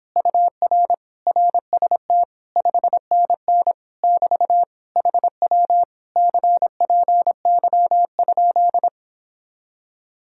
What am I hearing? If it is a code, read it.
Morse code, heard as a keyed tone, about 31 wpm, 700 Hz. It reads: UR RST 5NN <BT> HW CPY?